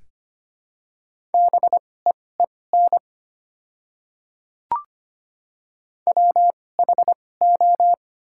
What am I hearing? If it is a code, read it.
Morse code, heard as a keyed tone, about 25 wpm, spaced as for 18 wpm, 700 Hz. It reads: BEEN WHO
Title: 2x Character Spacing - Top 500 Words - 25wpm